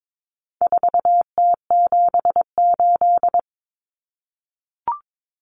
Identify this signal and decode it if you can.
Morse code, heard as a keyed tone, about 22 wpm, 700 Hz. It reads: 4T78
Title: Common Cut Numbers - 4-Digit 22wpm